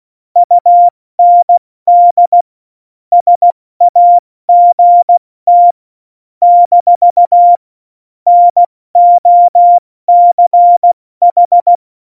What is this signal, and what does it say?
Morse code, heard as a keyed tone, about 16 wpm, 700 Hz. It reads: UND SAGT - NOCH
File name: Morsekurs / Witze / Witz 15. Oktober 2019, 70 F80